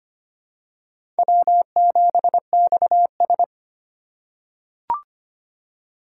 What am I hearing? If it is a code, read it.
Morse code, heard as a keyed tone, about 25 wpm, 700 Hz. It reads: W7XS
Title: USA Call Signs - 4 Characters - 3x Character Spacing - 25wpm